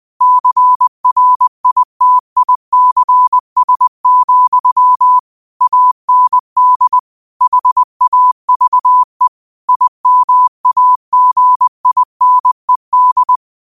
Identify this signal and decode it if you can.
Morse code, heard as a keyed tone, about 20 wpm, 1000 Hz. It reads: CRITICS, AND HAVE IMAGINED